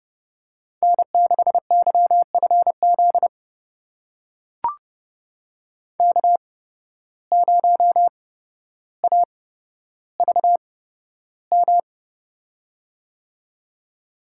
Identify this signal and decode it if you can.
Morse code, heard as a keyed tone, about 31 wpm, spaced as for 13 wpm, 700 Hz. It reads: N6YFZ K0AVM
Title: USA Call Signs - 5 Characters - 8x Character Spacing - 30wpm